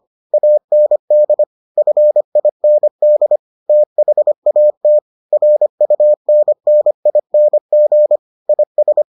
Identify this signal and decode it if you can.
Morse code, heard as a keyed tone, about 25 wpm, 600 Hz. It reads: AND FIND THAT RUNNING IS